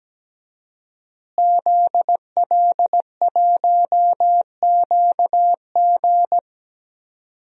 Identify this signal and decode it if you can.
Morse code, heard as a keyed tone, about 17 wpm, 700 Hz. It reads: ZL1QG
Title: DXCC-100 Call Signs - 5 Characters - Repeated 2x - 17wpm